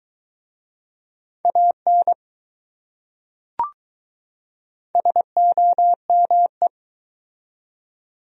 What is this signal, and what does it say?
Morse code, heard as a keyed tone, about 21 wpm, 700 Hz. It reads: AN SOME